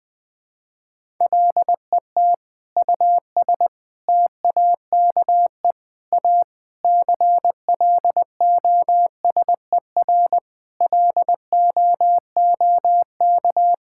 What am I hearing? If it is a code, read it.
Morse code, heard as a keyed tone, about 21 wpm, 700 Hz. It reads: LET US TAKE A CLOSER LOOK
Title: Sentences with 6-Word Phrases - 20wpm